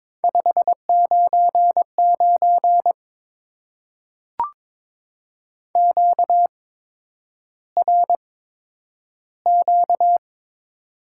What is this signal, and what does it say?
Morse code, heard as a keyed tone, about 22 wpm, spaced as for 9 wpm, 700 Hz. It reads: 599 QRQ
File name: Top 90 QSO Elements - 8x Character Spacing - 22wpm